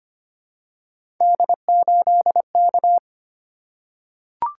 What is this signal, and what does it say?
Morse code, heard as a keyed tone, about 25 wpm, 700 Hz. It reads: D8K
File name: Three Letter-Number Combinations - 25wpm